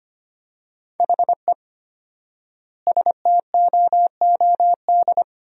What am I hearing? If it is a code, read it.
Morse code, heard as a keyed tone, about 25 wpm, 700 Hz. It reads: HE STOOD
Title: 4x Word-Spacing - Sentences from Top 500 Words - 25wpm